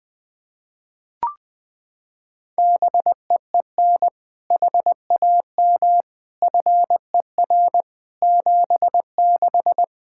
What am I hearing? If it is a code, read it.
Morse code, heard as a keyed tone, about 20 wpm, 700 Hz. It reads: BEEN HAM FER 76
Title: K1USN SST QSO Practice 20wpm